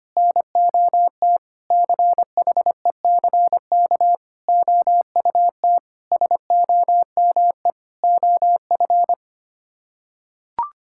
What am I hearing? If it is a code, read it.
Morse code, heard as a keyed tone, about 25 wpm, 700 Hz. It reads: NOT CHECK OUT SOME OF
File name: Sets of 9 Words - 25wpm